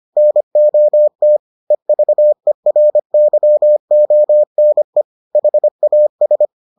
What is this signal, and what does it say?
Morse code, heard as a keyed tone, about 25 wpm, 600 Hz. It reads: NOT EVERYONE HAS